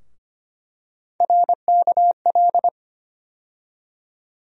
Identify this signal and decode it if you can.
Morse code, heard as a keyed tone, about 25 wpm, 700 Hz. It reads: RKL
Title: USA Call Sign Postfixes - 3 Letters 25wpm